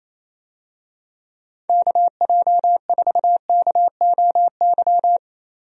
Morse code, heard as a keyed tone, about 27 wpm, 700 Hz. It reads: KJ4KOY
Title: USA Call Signs - 6 Characters - Repeated 3x - 28wpm